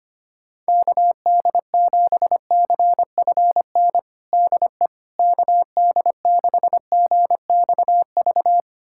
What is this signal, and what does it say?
Morse code, heard as a keyed tone, about 25 wpm, 700 Hz. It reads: KD7CFN DE KD6GXV
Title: USA Callsign DE Callsign 25wpm